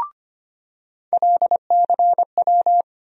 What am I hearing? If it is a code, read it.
Morse code, heard as a keyed tone, about 25 wpm, 700 Hz. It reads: LCW
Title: USA Call Sign Postfixes - 3 Letters 25wpm